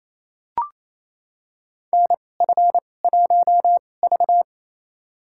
Morse code, heard as a keyed tone, about 28 wpm, spaced as for 20 wpm, 700 Hz. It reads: NF1V
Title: USA Call Signs - 4 Characters - 2x Character Spacing - 28wpm